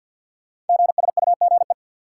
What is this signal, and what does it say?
Morse code, heard as a keyed tone, about 46 wpm, 700 Hz. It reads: GSUGE
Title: Code Groups - 5 Letters - 50wpm